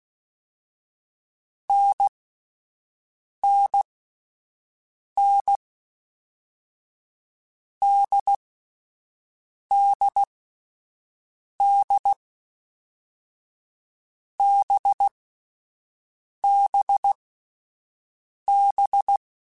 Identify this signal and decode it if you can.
Morse code, heard as a keyed tone, about 16 wpm, 800 Hz. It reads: N N N D D D B B B
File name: Short Runs / Short Runs-mp3s / NDB NDB6 @10